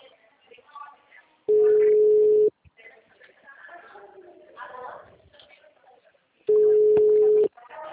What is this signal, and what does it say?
Ringback tone (Europe, 425 Hz), with speech.
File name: out- -2009- - - .227